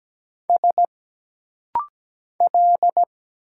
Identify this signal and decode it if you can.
Morse code, heard as a keyed tone, about 17 wpm, 700 Hz. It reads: S L
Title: Single Letter-Number Alt - Rapid-Fire 17wpm